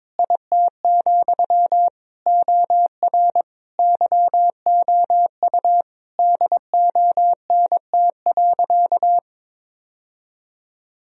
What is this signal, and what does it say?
Morse code, heard as a keyed tone, about 22 wpm, 700 Hz. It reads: IT, OR YOU DONT.
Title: Sentences from Top 1000 Words - Review Part 4 - 22wpm